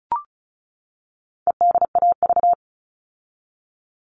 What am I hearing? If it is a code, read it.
Morse code, heard as a keyed tone, about 34 wpm, 700 Hz. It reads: EDAV
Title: All Cut Numbers - 4-Digit 35wpm